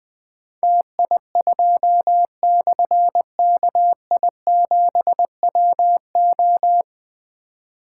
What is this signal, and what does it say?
Morse code, heard as a keyed tone, about 20 wpm, 700 Hz. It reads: TI2/KI7WO
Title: Difficult Call Signs 20wpm